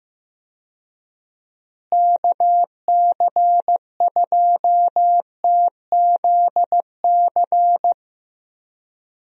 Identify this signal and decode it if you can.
Morse code, heard as a keyed tone, about 15 wpm, 700 Hz. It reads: KC2TZC
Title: USA Call Signs - 6 Characters - Repeated 2x - 15wpm